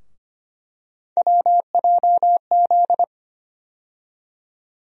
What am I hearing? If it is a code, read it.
Morse code, heard as a keyed tone, about 25 wpm, 700 Hz. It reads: WJZ
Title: USA Call Sign Postfixes - 3 Letters 25wpm